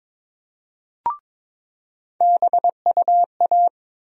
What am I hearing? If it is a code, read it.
Morse code, heard as a keyed tone, about 22 wpm, 700 Hz. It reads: BUA
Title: USA Call Sign Postfixes - 3 Letters 22wpm